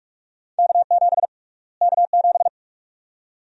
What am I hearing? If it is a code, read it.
Morse code, heard as a keyed tone, about 46 wpm, 700 Hz. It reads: K7 K7
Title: USA Call Sign Prefixes - Repeated 2x - 45wpm